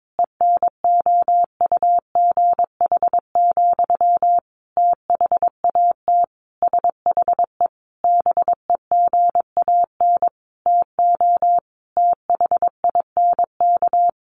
Morse code, heard as a keyed tone, about 22 wpm, 700 Hz. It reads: ENOUGH, THAT SHE BEGAN TO THINK
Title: Sentences from Top 800 Words 22wpm